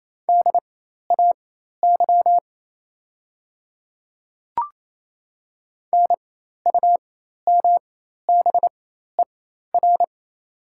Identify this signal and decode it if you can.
Morse code, heard as a keyed tone, about 28 wpm, spaced as for 13 wpm, 700 Hz. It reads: DAY NUMBER
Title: 4x Character Spacing - Top 100 Words - 28wpm